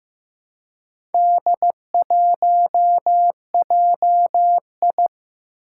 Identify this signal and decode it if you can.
Morse code, heard as a keyed tone, about 15 wpm, 700 Hz. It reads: D1JI
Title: Four Challenging Letter-Number Combinations 15wpm